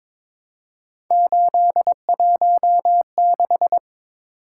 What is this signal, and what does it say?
Morse code, heard as a keyed tone, about 22 wpm, 700 Hz. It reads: 816
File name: practice / 22wpm / Three Letter-Number Combinations - 22wpm